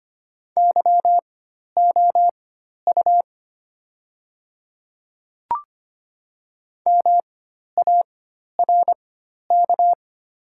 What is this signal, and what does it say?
Morse code, heard as a keyed tone, about 25 wpm, spaced as for 12 wpm, 700 Hz. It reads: YOU MARK
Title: 4x Character Spacing - Top 500 Words - 25wpm